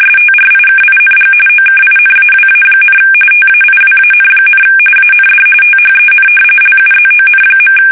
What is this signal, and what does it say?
RTTY, about 45.45 baud. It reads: STRIKING OKINAWA, TAIWAN, AND THEN LUZON.
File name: fsk850 / battleshipnj